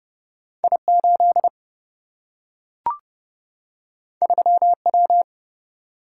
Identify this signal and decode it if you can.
Morse code, heard as a keyed tone, about 32 wpm, 700 Hz. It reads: I 8 3 W
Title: Single-Letter Single-Number 30wpm